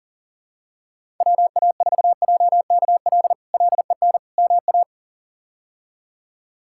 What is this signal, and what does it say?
Morse code, heard as a keyed tone, about 40 wpm, 700 Hz. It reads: WA4JKL LEN MA
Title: K1USN SST QSO Practice - 40wpm